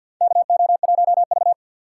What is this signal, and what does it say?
Morse code, heard as a keyed tone, about 53 wpm, 700 Hz. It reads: KO1V
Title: USA Call Signs - 4x Character Spacing - 50wpm